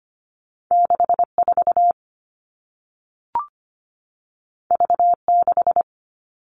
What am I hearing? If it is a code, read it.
Morse code, heard as a keyed tone, about 25 wpm, 700 Hz. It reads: 64 V6
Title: All Cut Numbers - 2-Digit 25wpm